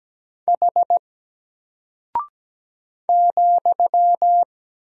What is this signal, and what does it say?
Morse code, heard as a keyed tone, about 17 wpm, 700 Hz. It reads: H ,
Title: Single Character (Letter:Number:Punctuation:Prosign) Alt 17wpm